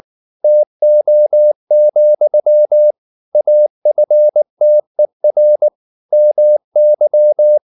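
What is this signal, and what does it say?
Morse code, heard as a keyed tone, about 19 wpm, 600 Hz. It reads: TO, AFTER MY
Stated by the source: Morse code at 19 wpm from CW med Gnister, reading Ken Follett's novel Never in English.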